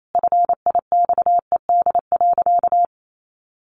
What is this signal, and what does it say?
Morse code, heard as a keyed tone, about 28 wpm, 700 Hz. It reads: FIXED.
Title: Sentences from Top 1100 Words 28wpm